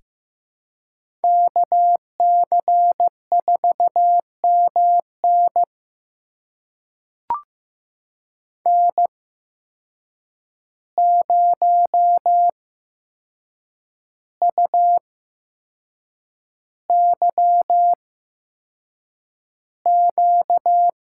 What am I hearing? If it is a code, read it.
Morse code, heard as a keyed tone, about 15 wpm, spaced as for 6 wpm, 700 Hz. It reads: KC4MN N0UYQ
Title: USA Call Signs - 5 Characters - 8x Character Spacing - 15wpm